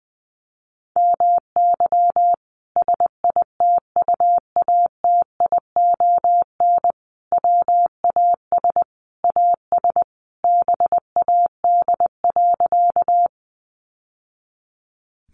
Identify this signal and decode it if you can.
Morse code, heard as a keyed tone, about 20 wpm, 700 Hz. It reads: MY SITUATION WAS AS BAD.
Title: Sentences from Top 1200 Words 20wpm